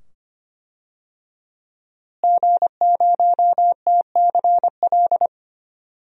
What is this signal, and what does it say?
Morse code, heard as a keyed tone, about 25 wpm, 700 Hz. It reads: G0TCL
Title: DXCC-100 Call Signs - 5 Characters - Repeated 2x - 25wpm